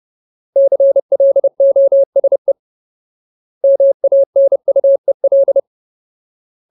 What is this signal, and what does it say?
Morse code, heard as a keyed tone, about 30 wpm, 550 Hz. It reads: CLOSE MANUEL
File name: Ultimate Word List / 30WPM Ultimate